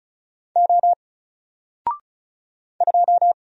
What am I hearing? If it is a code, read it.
Morse code, heard as a keyed tone, about 35 wpm, 700 Hz. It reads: O 2
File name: Single-Letter-Number-35wpm